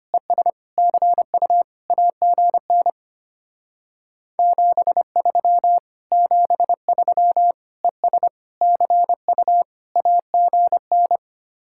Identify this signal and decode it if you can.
Morse code, heard as a keyed tone, about 32 wpm, 690 Hz. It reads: ES CU AGN 73 73 ES CU AGN